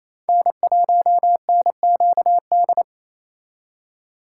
Morse code, heard as a keyed tone, about 34 wpm, 700 Hz. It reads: N1NQD
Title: Speed Racing - USA 1x3 Call Signs 28wpm